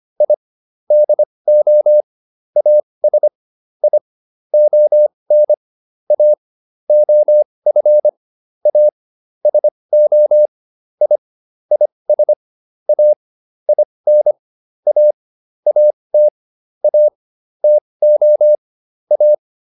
Morse code, heard as a keyed tone, about 25 wpm, spaced as for 20 wpm, 600 Hz. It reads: I DO AS I ON A OF A SO I IS A IN A AT A TO A